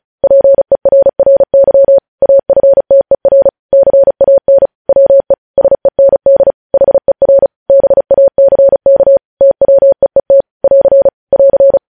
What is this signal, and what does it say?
Morse code, heard as a keyed tone, about 36 wpm, 550 Hz. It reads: PERRY AFTER CAN WE SEND HER BACK TWEET <AR> <AR>
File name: podcast / News headlines-1x-35x35